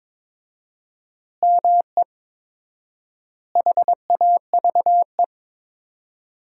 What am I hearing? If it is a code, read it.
Morse code, heard as a keyed tone, about 22 wpm, 700 Hz. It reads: ME HAVE
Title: 4x Word-Spacing - Sentences from Top 300 Words - 22wpm